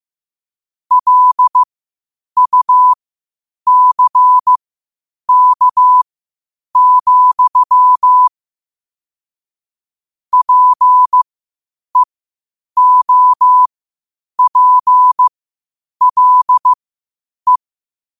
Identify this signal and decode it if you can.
Morse code, heard as a keyed tone, about 15 wpm, spaced as for 8 wpm, 1000 Hz. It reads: LUCK, PEOPLE
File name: qotdfiles / 8wpm